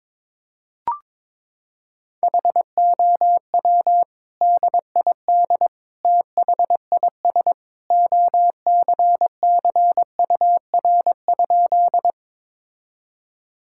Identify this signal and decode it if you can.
Morse code, heard as a keyed tone, about 22 wpm, 700 Hz. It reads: HOW DID THIS OCCUR?
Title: Sentences from Top 1000 Words - Review Part 6 - 22wpm